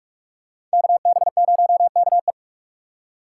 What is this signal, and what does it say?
Morse code, heard as a keyed tone, about 43 wpm, 700 Hz. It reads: KB0KE
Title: USA Call Signs - 4x Character Spacing - 45wpm